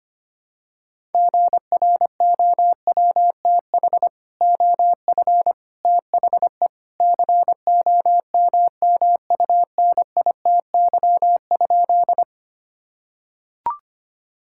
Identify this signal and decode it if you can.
Morse code, heard as a keyed tone, about 25 wpm, 700 Hz. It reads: GROWTH OF THE COMMUNITY?
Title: Sentences from Top 1000 Words - Review Part 5 - 25wpm